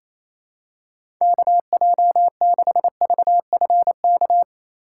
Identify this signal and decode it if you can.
Morse code, heard as a keyed tone, about 28 wpm, 700 Hz. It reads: KJ6VFK
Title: USA Call Signs - 6 Characters - 2x Character Spacing - 28wpm